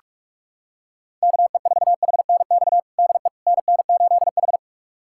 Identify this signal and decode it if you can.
Morse code, heard as a keyed tone, about 46 wpm, 700 Hz. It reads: KE4HNX DE NN8H
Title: USA Callsign DE Callsign 45wpm